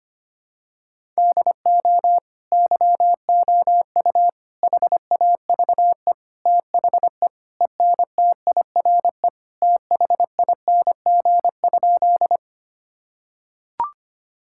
Morse code, heard as a keyed tone, about 25 wpm, 700 Hz. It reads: DO YOU HAVE THE ENTIRE THING?
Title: Sentences from Top 1000 Words - Review Part 5 - 25wpm